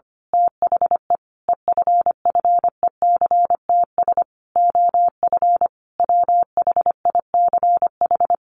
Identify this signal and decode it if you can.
Morse code, heard as a keyed tone, about 25 wpm, 700 Hz. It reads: THE EFFECTS OF WHICH
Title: A Sound Of Thunder 25wpm